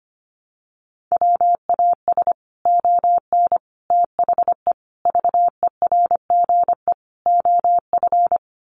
Morse code, heard as a keyed tone, about 25 wpm, 700 Hz. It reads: WAS ON THE VERGE OF
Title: Sets of 5 Words Encore 25wpm